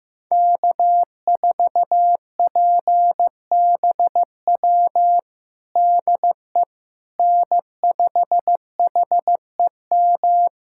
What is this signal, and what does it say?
Morse code, heard as a keyed tone, about 15 wpm, 700 Hz. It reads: K4PBW DE N5HEM
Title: USA Callsign DE Callsign - 5 Characters - 15wpm